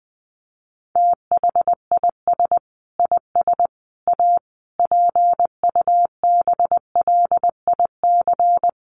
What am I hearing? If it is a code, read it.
Morse code, heard as a keyed tone, about 20 wpm, 700 Hz. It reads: THIS IS A PUBLIC
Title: Sentences from Top 600 Words 20wpm